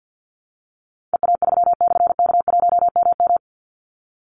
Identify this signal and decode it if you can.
Morse code, heard as a keyed tone, about 50 wpm, 700 Hz. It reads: EA3/K1MM
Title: Call Signs - 50wpm